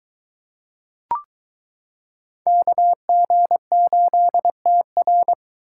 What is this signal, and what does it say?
Morse code, heard as a keyed tone, about 22 wpm, 700 Hz. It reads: KG8TR